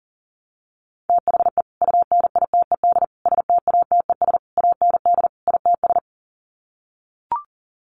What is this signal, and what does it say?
Morse code, heard as a keyed tone, about 40 wpm, 700 Hz. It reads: THE UNITED STATES AND ITS
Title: Sets of 5 Words Encore 40wpm